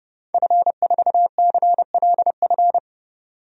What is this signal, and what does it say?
Morse code, heard as a keyed tone, about 30 wpm, 690 Hz. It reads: F4CLF
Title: DXCC-100 Call Signs - 5 Characters - 30wpm